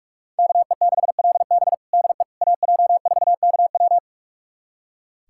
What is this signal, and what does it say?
Morse code, heard as a keyed tone, about 46 wpm, 700 Hz. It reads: KE6LB DE AJ4KW
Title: USA Callsign DE Callsign - 5 Characters - 45wpm